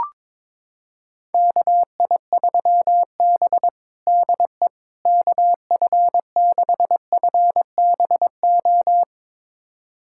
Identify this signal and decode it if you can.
Morse code, heard as a keyed tone, about 22 wpm, 700 Hz. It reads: KI3B DE KF6FBO
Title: USA Callsign DE Callsign 22wpm